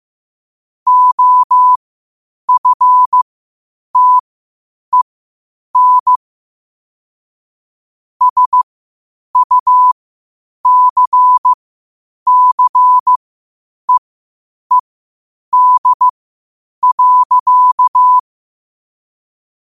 Morse code, heard as a keyed tone, about 15 wpm, spaced as for 8 wpm, 1000 Hz. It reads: OFTEN SUCCEED.
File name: qotdfiles / 8wpm